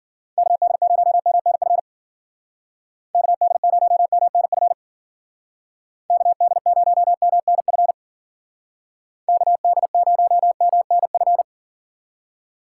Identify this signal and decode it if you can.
Morse code, heard as a keyed tone, about 50 wpm, 700 Hz. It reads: KD0MNF KD0MNF KD0MNF KD0MNF
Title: Speed Racing - USA Call Signs 40wpm